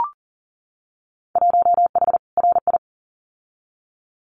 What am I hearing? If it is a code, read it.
Morse code, heard as a keyed tone, about 41 wpm, 700 Hz. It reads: 1H RI
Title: Winter Field Day Exchanges 40wpm